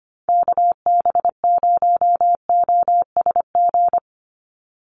Morse code, heard as a keyed tone, about 25 wpm, 700 Hz. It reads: KB0OSG
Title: USA Call Signs 25wpm